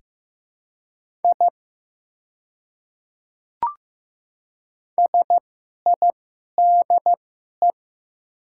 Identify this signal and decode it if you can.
Morse code, heard as a keyed tone, about 15 wpm, spaced as for 11 wpm, 700 Hz. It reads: I SIDE
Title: 2x Character Spacing - Top 100 Words - 15wpm